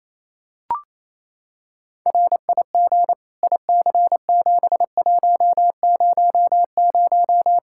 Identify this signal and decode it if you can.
Morse code, heard as a keyed tone, about 28 wpm, 700 Hz. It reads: RIG IC7100
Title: Practice QSO - Exchange 2 - Info - 28wpm